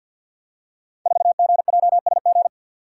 Morse code, heard as a keyed tone, about 52 wpm, 700 Hz. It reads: 4GJSG
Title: Code Groups - 5 Letter:Numbers - 50wpm